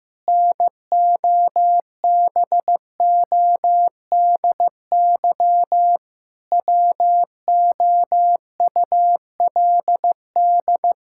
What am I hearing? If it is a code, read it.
Morse code, heard as a keyed tone, about 15 wpm, 700 Hz. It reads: NOBODY WOULD
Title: Sentences from Top 1600 Words 15wpm